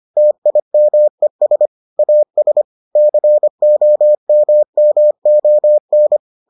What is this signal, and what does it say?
Morse code, heard as a keyed tone, about 25 wpm, 600 Hz. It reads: TIMES AS COMMON